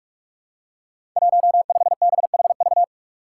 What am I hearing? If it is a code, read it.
Morse code, heard as a keyed tone, about 44 wpm, 700 Hz. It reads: 15BHV
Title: Five Challenging Letter-Number Combinations 45wpm